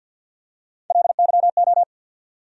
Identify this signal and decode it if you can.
Morse code, heard as a keyed tone, about 52 wpm, 700 Hz. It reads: LYO
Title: USA Call Sign Postfixes - 3 Letters 50wpm